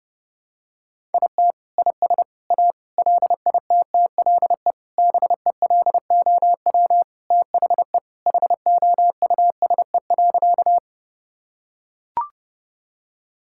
Morse code, heard as a keyed tone, about 32 wpm, 700 Hz. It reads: IT IS A LITTLE BELOW THE HOUSE.
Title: Sentences from Top 1000 Words - Review Part 5 - 30wpm